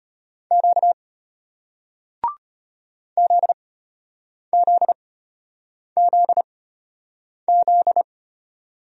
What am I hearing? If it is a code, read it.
Morse code, heard as a keyed tone, about 33 wpm, spaced as for 8 wpm, 700 Hz. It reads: Q ZZZZ